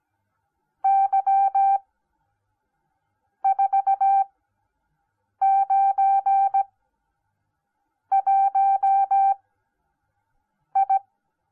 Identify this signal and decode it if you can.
Morse code, heard as a keyed tone, about 17 wpm, spaced as for 5 wpm, 780 Hz. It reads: Y491I